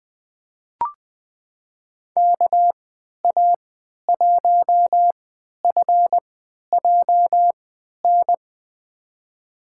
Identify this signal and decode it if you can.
Morse code, heard as a keyed tone, about 20 wpm, spaced as for 11 wpm, 700 Hz. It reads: KA1FJN
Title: USA Call Signs - 6 Characters - 3x Character Spacing - 20wpm